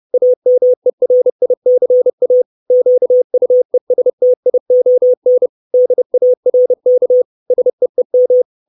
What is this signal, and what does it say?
Morse code, heard as a keyed tone, about 30 wpm, 500 Hz. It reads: AMERICA QUESTION DARK SEEM
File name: 632 Words 30WPM